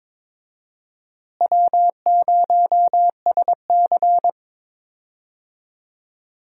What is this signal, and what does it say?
Morse code, heard as a keyed tone, about 22 wpm, 700 Hz. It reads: W0SC
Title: USA Call Signs - 4 Characters - Repeated 3x - 22wpm